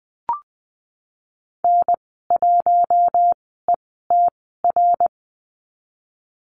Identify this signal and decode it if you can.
Morse code, heard as a keyed tone, about 19 wpm, spaced as for 14 wpm, 700 Hz. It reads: N1ETR
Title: 062 Lesson 11g - Letter h - Call Signs 20wpm